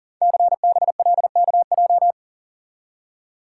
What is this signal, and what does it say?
Morse code, heard as a keyed tone, about 38 wpm, 700 Hz. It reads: CBLKJ
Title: Code Groups - 5 Letters - 40wpm